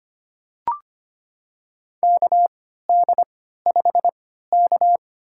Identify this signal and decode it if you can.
Morse code, heard as a keyed tone, about 25 wpm, spaced as for 14 wpm, 700 Hz. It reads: KD5K
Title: USA Call Signs - 4 Characters - 3x Character Spacing - 25wpm